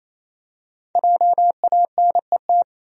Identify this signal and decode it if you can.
Morse code, heard as a keyed tone, about 28 wpm, 700 Hz. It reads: JANET
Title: USA Common Names - Max 5 Letters - 28wpm